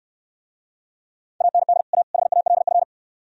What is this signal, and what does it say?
Morse code, heard as a keyed tone, about 69 wpm, 700 Hz. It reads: AND A HALF